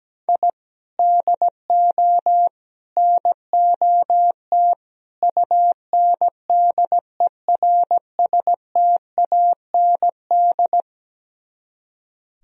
Morse code, heard as a keyed tone, about 17 wpm, 700 Hz. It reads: I DO NOT UNDERSTAND
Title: Speed-Racing - Sets of 4 Words 17wpm